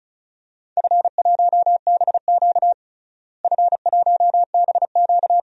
Morse code, heard as a keyed tone, about 34 wpm, 700 Hz. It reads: F1BQ F1BQ
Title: DXCC-100 Call Signs - 4 Characters - Repeated 2x - 35wpm